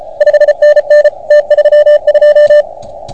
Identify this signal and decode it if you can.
Morse code, heard as a keyed tone, about 32 wpm, 590 Hz. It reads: 5NN T32
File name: Archiv / CUC381 / OK1FGD CUC 381